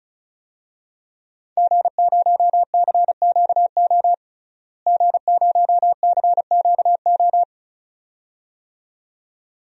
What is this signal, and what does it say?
Morse code, heard as a keyed tone, about 36 wpm, 700 Hz. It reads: G0CQO G0CQO
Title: DXCC-100 Call Signs - 5 Characters - Repeated 2x - 35wpm